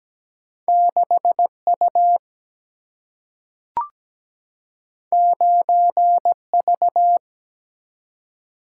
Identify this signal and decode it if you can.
Morse code, heard as a keyed tone, about 17 wpm, 700 Hz. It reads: 6U 9V
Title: Single-Letter Single-Number 17wpm